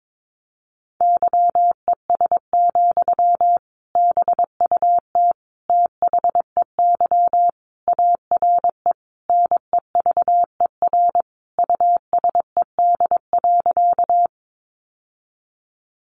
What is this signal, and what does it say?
Morse code, heard as a keyed tone, about 22 wpm, 700 Hz. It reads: YES, BUT THEY ARE NEVER USED.
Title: Sentences from Top 600 Words 22wpm